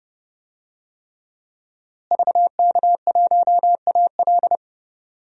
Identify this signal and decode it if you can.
Morse code, heard as a keyed tone, about 32 wpm, 700 Hz. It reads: VK1AL
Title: DXCC-100 Call Signs - 5 Characters - Repeated 3x - 30wpm